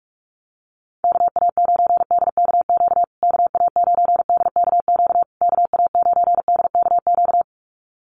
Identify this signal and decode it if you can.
Morse code, heard as a keyed tone, about 44 wpm, 700 Hz. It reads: KA9DKQ KA9DKQ KA9DKQ
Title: USA Call Signs Repeated 3x 45wpm